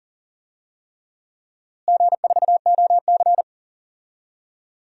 Morse code, heard as a keyed tone, about 40 wpm, 700 Hz. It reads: G4OC
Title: DXCC-100 Call Signs - 4 Characters - Repeated 3x - 40wpm